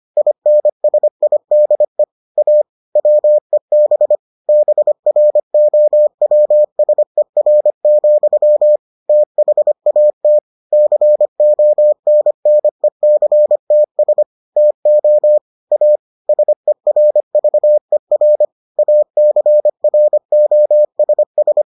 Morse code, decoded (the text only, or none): INSIDE A WEB BROWSER, THAT CONNECTS TO A SERVER ACROSS